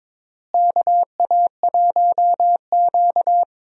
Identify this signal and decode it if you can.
Morse code, heard as a keyed tone, about 22 wpm, 700 Hz. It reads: KA1Q
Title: USA Call Signs - 4 Characters - Repeated 2x - 22wpm